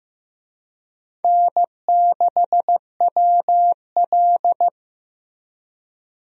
Morse code, heard as a keyed tone, about 15 wpm, 700 Hz. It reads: N6WL
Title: USA Call Signs - 4 Characters - Repeated 3x - 15wpm